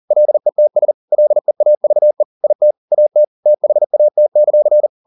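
Morse code, heard as a keyed tone, about 40 wpm, 600 Hz. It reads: LETS LEAVE IT AT THAT;
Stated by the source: Morse code at 40 wpm, practice text, CW med Gnister.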